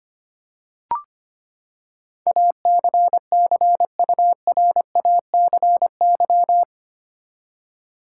Morse code, heard as a keyed tone, about 25 wpm, 700 Hz. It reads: ACCURACY
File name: Eight Letter Words - 25wpm